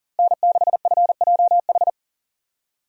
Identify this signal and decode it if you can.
Morse code, heard as a keyed tone, about 40 wpm, 700 Hz. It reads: N6FJH
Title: USA Call Signs - 5 Characters - Repeated 3x - 40wpm